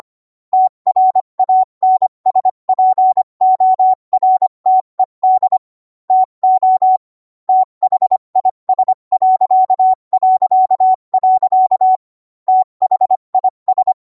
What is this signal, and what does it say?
Morse code, heard as a keyed tone, about 25 wpm, 750 Hz. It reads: TRANSPORTED TO THIS... THIS